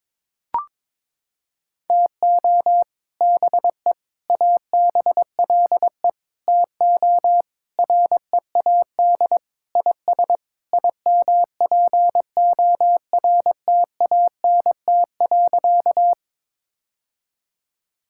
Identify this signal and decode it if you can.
Morse code, heard as a keyed tone, about 21 wpm, 700 Hz. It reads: TO BE ABLE TO READ IS IMPORTANT.
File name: practice / 22wpm / Sentences with 3-Word Phrases - 22wpm